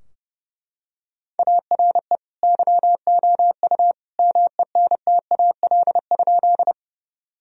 Morse code, heard as a keyed tone, about 28 wpm, 700 Hz. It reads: ARE YOU MENTAL?
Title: Sentences from Top 1000 Words - Review Part 4 - 30wpm